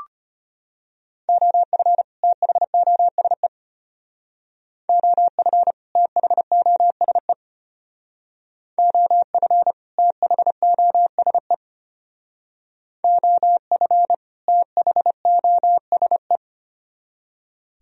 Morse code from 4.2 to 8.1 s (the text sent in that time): OF THOSE